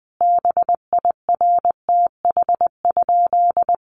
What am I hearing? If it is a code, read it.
Morse code, heard as a keyed tone, about 20 wpm, 700 Hz. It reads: BIRTH?
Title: Sentences from Top 800 Words 20wpm